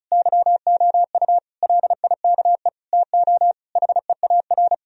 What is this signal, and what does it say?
Morse code, heard as a keyed tone, about 34 wpm, 700 Hz. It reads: YOU LIKE TO HEAR
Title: Sentences from Top 1000 Words - Review Part 1 - 35wpm